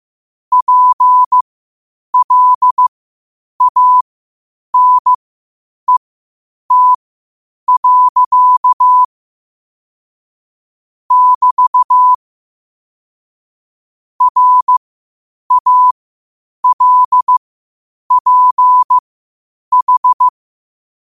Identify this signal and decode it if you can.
Morse code, heard as a keyed tone, about 15 wpm, spaced as for 8 wpm, 1000 Hz. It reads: PLANET. <BT> RALPH
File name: 8wpm